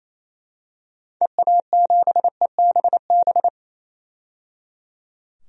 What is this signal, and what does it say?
Morse code, heard as a keyed tone, about 28 wpm, 700 Hz. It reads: EA7EBB
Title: DXCC-100 Call Signs - 6 Characters - 28wpm